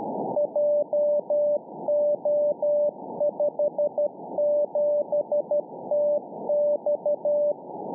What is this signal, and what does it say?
Morse code, heard as a keyed tone, about 12 wpm, 590 Hz. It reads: JO57TX